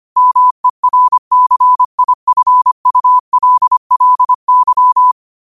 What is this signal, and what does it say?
Morse code, heard as a keyed tone, about 25 wpm, 1000 Hz. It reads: MERCIFULLY